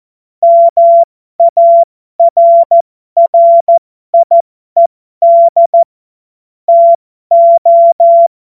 Morse code, heard as a keyed tone, about 14 wpm, spaced as for 12 wpm, 680 Hz. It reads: MARRIED TO